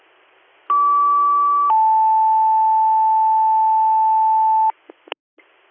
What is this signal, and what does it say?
Two-tone page, heard as a one-second tone, then a three-second tone, about 1150 then 880 Hz. It is heard from 0.7 to 4.7 s.